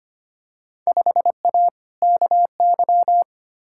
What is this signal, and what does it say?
Morse code, heard as a keyed tone, about 25 wpm, 700 Hz. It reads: 5A KY
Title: Field Day Exchanges - 25wpm